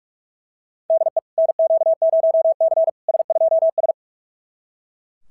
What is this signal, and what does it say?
Morse code, heard as a keyed tone, about 44 wpm, 650 Hz. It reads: DE NQ0C S2S
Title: SOTA QSO Practice 45wpm